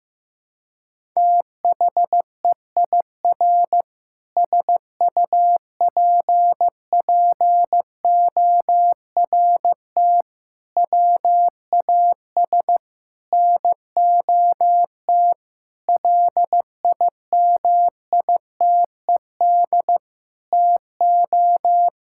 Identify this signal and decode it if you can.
Morse code, heard as a keyed tone, about 15 wpm, 700 Hz. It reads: THEIR SUPPORT WAS NOT LIMITED TO
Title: Sentences from Top 1000 Words - Review Part 6 - 15wpm